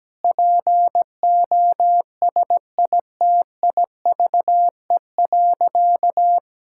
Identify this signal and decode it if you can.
Morse code, heard as a keyed tone, about 17 wpm, 700 Hz. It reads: POSITIVE.
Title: Sentences from Top 1000 Words - Review Part 5 - 17wpm